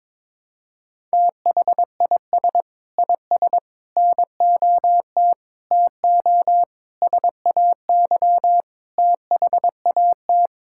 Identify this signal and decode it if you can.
Morse code, heard as a keyed tone, about 22 wpm, 700 Hz. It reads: THIS IS NOT TO SAY THAT
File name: Sentences with 7-Word Phrases - 22wpm